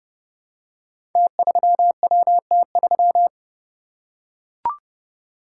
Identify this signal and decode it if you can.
Morse code, heard as a keyed tone, about 32 wpm, 700 Hz. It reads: T3WT3
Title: Code Groups - 5 Letter:Numbers - 30wpm